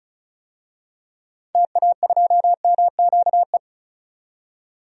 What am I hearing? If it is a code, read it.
Morse code, heard as a keyed tone, about 36 wpm, 700 Hz. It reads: TA2MQE
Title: DXCC-100 Call Signs - 6 Characters - 35wpm